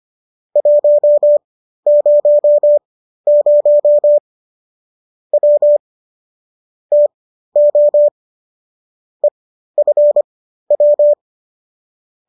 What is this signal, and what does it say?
Morse code, heard as a keyed tone, about 25 wpm, spaced as for 13 wpm, 590 Hz. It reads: 100 W TO EFW